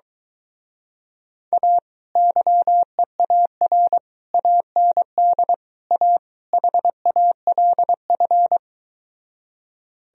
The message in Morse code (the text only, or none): A YEAR AND A HALF